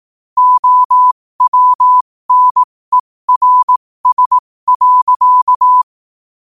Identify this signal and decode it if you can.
Morse code, heard as a keyed tone, about 18 wpm, spaced as for 15 wpm, 1000 Hz. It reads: OWNERS.
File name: qotdfiles / 15wpm